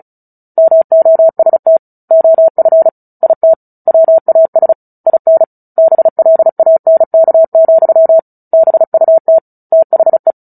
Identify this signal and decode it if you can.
Morse code, heard as a keyed tone, about 37 wpm, 650 Hz. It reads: MOST OF IT WAS IN BLANK, BUT THE
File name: Book Two / 09-Chapter 9 - 35 WPM